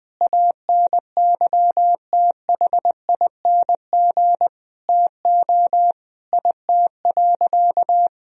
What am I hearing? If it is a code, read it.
Morse code, heard as a keyed tone, about 20 wpm, 700 Hz. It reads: ANYTHING TO IT.
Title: Sentences from Top 1000 Words - Review Part 1 - 20wpm